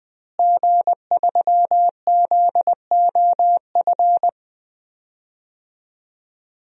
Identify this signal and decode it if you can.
Morse code, heard as a keyed tone, about 20 wpm, 700 Hz. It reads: G3ZOF
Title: DXCC-100 Call Signs - 5 Characters - Repeated 2x - 20wpm